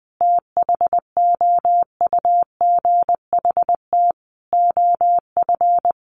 Morse code, heard as a keyed tone, about 20 wpm, 700 Hz. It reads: THOUGHT OF
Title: Sentences from Top 1300 Words 20wpm